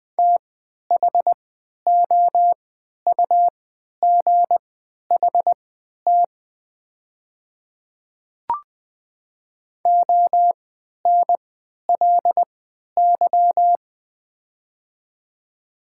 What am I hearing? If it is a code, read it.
Morse code, heard as a keyed tone, about 20 wpm, spaced as for 11 wpm, 700 Hz. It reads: THOUGHT ONLY
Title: 3x Character Spacing - Top 300 Words - 20wpm